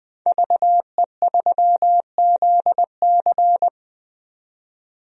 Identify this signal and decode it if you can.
Morse code, heard as a keyed tone, about 20 wpm, 700 Hz. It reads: VE3ZC
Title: DXCC-100 Call Signs - 5 Characters - 20wpm